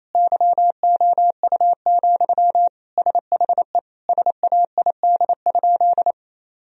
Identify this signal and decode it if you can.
Morse code, heard as a keyed tone, about 28 wpm, 700 Hz. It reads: YOU, SHE SAID?
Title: Sentences from Top 1000 Words - Review Part 1 - 28wpm